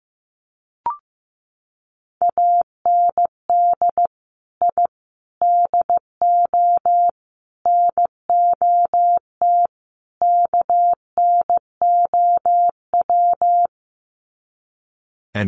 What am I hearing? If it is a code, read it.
Morse code, heard as a keyed tone, about 15 wpm, 700 Hz. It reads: AND I DO NOT KNOW
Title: Sets of 5 Words 15wpm